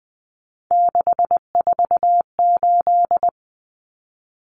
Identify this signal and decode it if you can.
Morse code, heard as a keyed tone, about 20 wpm, 700 Hz. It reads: 648
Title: Numbers - 3-Digit 20wpm